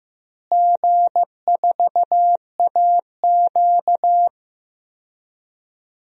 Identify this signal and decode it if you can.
Morse code, heard as a keyed tone, about 15 wpm, 700 Hz. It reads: G4AQ
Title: DXCC-100 Call Signs - 4 Characters - Repeated 2x - 15wpm